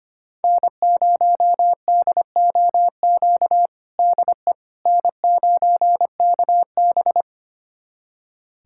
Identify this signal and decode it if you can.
Morse code, heard as a keyed tone, about 25 wpm, 700 Hz. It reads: N0DOQ DE N9KB
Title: USA Callsign DE Callsign 25wpm